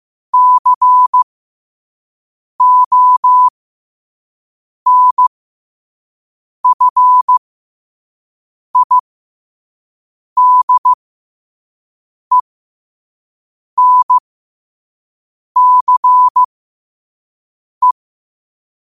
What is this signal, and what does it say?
Morse code, heard as a keyed tone, about 15 wpm, spaced as for 5 wpm, 1000 Hz. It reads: CONFIDENCE
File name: qotdfiles / 5wpm